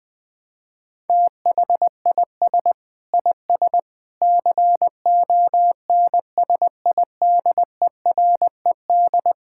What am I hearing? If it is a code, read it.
Morse code, heard as a keyed tone, about 20 wpm, 700 Hz. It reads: THIS IS CONSIDERED
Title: Sentences with 7-Word Phrases - 20wpm